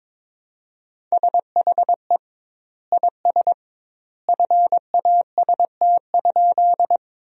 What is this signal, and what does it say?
Morse code, heard as a keyed tone, about 22 wpm, 700 Hz. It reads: SHE IS FAST?
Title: 2x Word-Spacing - Sentences from Top 400 Words - 22wpm